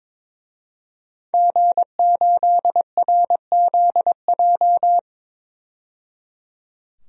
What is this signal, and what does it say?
Morse code, heard as a keyed tone, about 23 wpm, 700 Hz. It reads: G8RZJ
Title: DXCC-100 Call Signs - 5 Characters - 22wpm